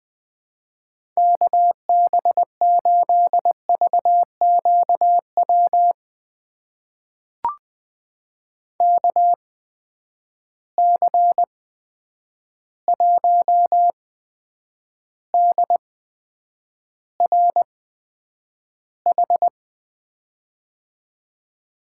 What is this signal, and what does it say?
Morse code, heard as a keyed tone, about 21 wpm, spaced as for 9 wpm, 700 Hz. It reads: KB8VQW KC1DRH
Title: USA Call Signs - 8x Character Spacing - 20wpm